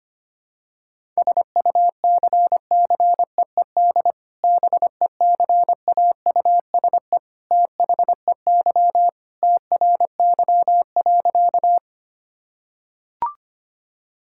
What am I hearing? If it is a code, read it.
Morse code, heard as a keyed tone, about 25 wpm, 700 Hz. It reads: SUCCEED BECAUSE THEY TRY.
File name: practice / 25wpm / Sentences with 2-Word Phrases - 25wpm